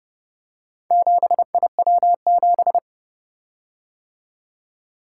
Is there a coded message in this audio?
Yes: Morse code from 0.9 to 2.8 s, heard as a keyed tone, about 29 wpm, 700 Hz. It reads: 7IW7